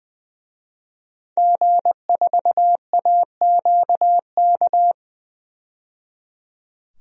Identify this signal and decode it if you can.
Morse code, heard as a keyed tone, about 20 wpm, 700 Hz. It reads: G4AQK
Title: DXCC-100 Call Signs - 5 Characters - 20wpm